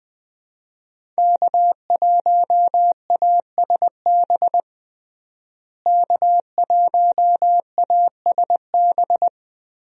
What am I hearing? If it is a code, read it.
Morse code, heard as a keyed tone, about 21 wpm, 700 Hz. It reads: K1ASB K1ASB
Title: USA Call Signs - 5 Characters - Repeated 3x - 20wpm